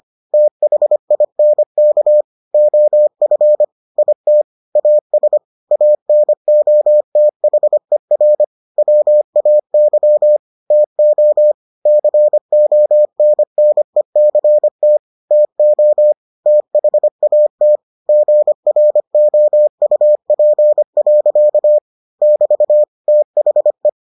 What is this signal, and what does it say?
Morse code, heard as a keyed tone, about 25 wpm, 600 Hz. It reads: THINK OF IT AS ANOTHER WAY TO CONNECT TO THAT GROUP. <BT> THE